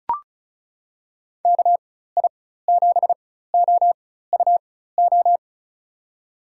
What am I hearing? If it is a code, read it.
Morse code, heard as a keyed tone, about 36 wpm, spaced as for 16 wpm, 700 Hz. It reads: KI7OUO
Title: USA Call Signs - 4x Character Spacing - 35wpm